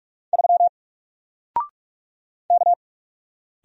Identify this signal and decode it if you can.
Morse code, heard as a keyed tone, about 44 wpm, 700 Hz. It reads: 3 K
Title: Single Letter-Number Alt 45wpm